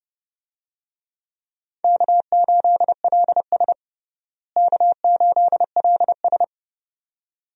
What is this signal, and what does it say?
Morse code, heard as a keyed tone, about 30 wpm, 700 Hz. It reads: K8LS K8LS
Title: USA Call Signs - 4 Characters - Repeated 2x - 30wpm